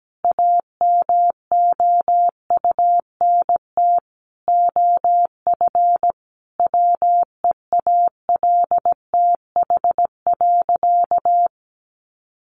Morse code, heard as a keyed tone, about 17 wpm, 700 Hz. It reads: AMOUNT OF WEALTH.
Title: Sentences from Top 1300 Words 17wpm